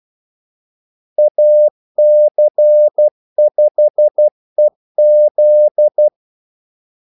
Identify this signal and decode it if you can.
Morse code, heard as a keyed tone, about 12 wpm, 600 Hz. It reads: AC5EZ
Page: Kaldesignaler 12 wpm